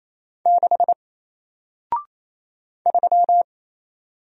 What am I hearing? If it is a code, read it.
Morse code, heard as a keyed tone, about 29 wpm, 700 Hz. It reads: 6 3
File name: Single-Letter-Number-28wpm